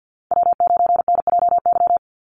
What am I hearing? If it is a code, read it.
Morse code, heard as a keyed tone, about 48 wpm, 700 Hz. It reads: W9NJY
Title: Call Signs Repeated 3x - 50wpm